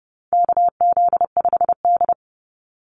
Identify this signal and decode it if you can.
Morse code, heard as a keyed tone, about 30 wpm, 700 Hz. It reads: KZ5D
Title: Call Signs Repeated 2x - 30wpm